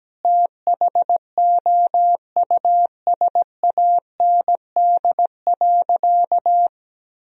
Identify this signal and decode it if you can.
Morse code, heard as a keyed tone, about 17 wpm, 700 Hz. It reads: THOUSAND.
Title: Sentences from Top 1000 Words - Review Part 3 - 17wpm